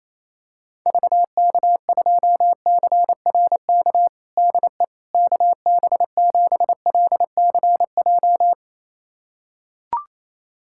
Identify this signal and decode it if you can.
Morse code, heard as a keyed tone, about 28 wpm, 700 Hz. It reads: VK2CRK DE KB7LCJ
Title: USA Callsign DE Callsign 28wpm